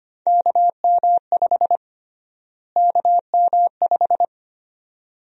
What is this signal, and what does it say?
Morse code, heard as a keyed tone, about 25 wpm, 700 Hz. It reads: KM5 KM5
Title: USA Call Sign Prefixes - Repeated 2x - 25wpm